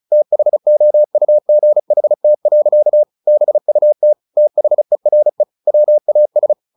Morse code, heard as a keyed tone, about 34 wpm, 600 Hz. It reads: THOUGHT. BUT THERE WAS
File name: the old man and the sea 2 35wpm